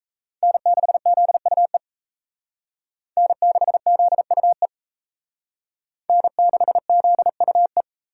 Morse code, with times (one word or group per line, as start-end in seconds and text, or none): 0.4-1.8 N6ZUE
3.2-4.7 N6ZUE
6.1-7.8 N6ZUE